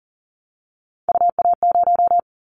Morse code, heard as a keyed tone, about 41 wpm, 700 Hz. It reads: UA0
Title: Callsign Prefixes for Top 50 DXCC - 40wpm